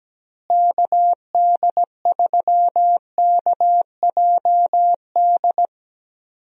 Morse code, heard as a keyed tone, about 17 wpm, 700 Hz. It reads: KD3KJD
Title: USA Call Signs - 6 Characters - Repeated 2x - 17wpm